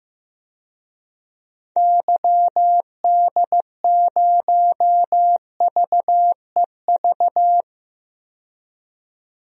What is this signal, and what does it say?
Morse code, heard as a keyed tone, about 15 wpm, 700 Hz. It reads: YD0VEV
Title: DXCC-100 Call Signs - 6 Characters - 15wpm